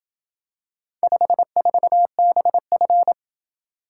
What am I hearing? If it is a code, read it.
Morse code, heard as a keyed tone, about 27 wpm, 700 Hz. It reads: 54BF